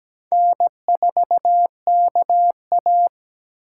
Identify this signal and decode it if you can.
Morse code, heard as a keyed tone, about 20 wpm, 700 Hz. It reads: N4KA
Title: Speed Racing - USA 1x2 Call Signs 17wpm